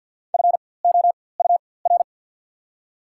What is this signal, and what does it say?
Morse code, heard as a keyed tone, about 54 wpm, spaced as for 24 wpm, 690 Hz. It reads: FOUR
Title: 4x Character Spacing - Top 500 Words - 50wpm